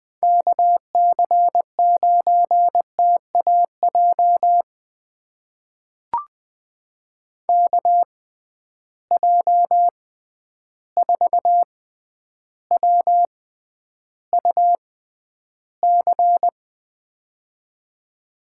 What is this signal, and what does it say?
Morse code, heard as a keyed tone, about 20 wpm, spaced as for 10 wpm, 700 Hz. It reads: KC9TAJ KJ4WUC
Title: USA Call Signs - 6 Characters - 6x Character Spacing - 20wpm